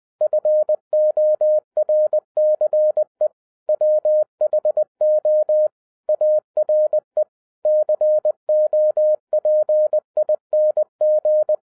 Morse code, heard as a keyed tone, about 20 wpm, 610 Hz. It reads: FORCE WHO ARE COPING